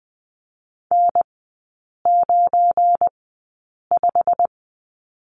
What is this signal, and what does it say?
Morse code, heard as a keyed tone, about 20 wpm, spaced as for 8 wpm, 700 Hz. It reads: N95
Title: 178 Lesson 29e - Slash - Three Characters 20wpm